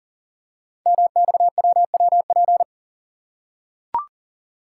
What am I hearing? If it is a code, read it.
Morse code, heard as a keyed tone, about 42 wpm, 700 Hz. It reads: MXWWP
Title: Code Groups - 5 Letters - 40wpm